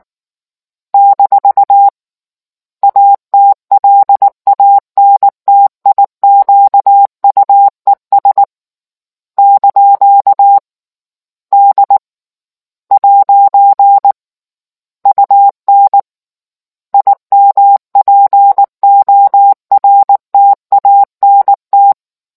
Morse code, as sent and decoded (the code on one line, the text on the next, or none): -....- / .- - .-.. .- -. - .. --.- ..- . ... / -.--.- / -.. / .----. / ..- -. / .. -- .--. --- .-. - .- -. -
- ATLANTIQUES ) D ' UN IMPORTANT